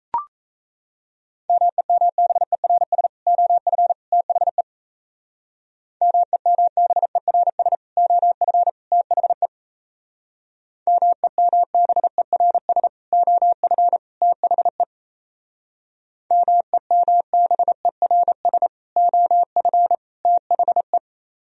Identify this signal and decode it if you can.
Morse code, heard as a keyed tone, about 34 wpm, 700 Hz. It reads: MEMBERS OF THE MEMBERS OF THE MEMBERS OF THE MEMBERS OF THE
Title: Speed-Racing - Sets of 3 Words 28wpm